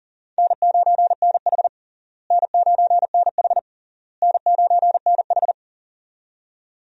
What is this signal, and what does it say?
Morse code, heard as a keyed tone, about 40 wpm, 700 Hz. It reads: N9NH N9NH N9NH
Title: USA Call Signs - 4 Characters - Repeated 3x - 40wpm